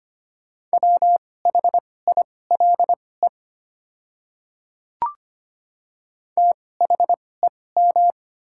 Morse code, heard as a keyed tone, about 25 wpm, spaced as for 18 wpm, 700 Hz. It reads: WHILE THEM
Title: 2x Character Spacing - Top 300 Words - 25wpm